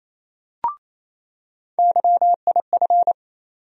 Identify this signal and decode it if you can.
Morse code, heard as a keyed tone, about 28 wpm, 700 Hz. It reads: YIF
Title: USA Call Sign Postfixes - 3 Letters 28wpm